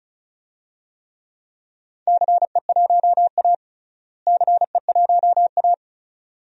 Morse code, heard as a keyed tone, about 36 wpm, 700 Hz. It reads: CE1A CE1A
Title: DXCC-100 Call Signs - 4 Characters - Repeated 2x - 35wpm